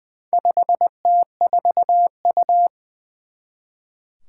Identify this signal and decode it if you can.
Morse code, heard as a keyed tone, about 20 wpm, 700 Hz. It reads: 5T4U
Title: Four Challenging Letter-Number Combinations 20wpm